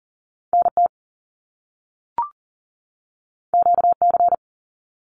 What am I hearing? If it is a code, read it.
Morse code, heard as a keyed tone, about 40 wpm, 700 Hz. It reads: NT QC
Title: Canadian Provinces and Territories 40wpm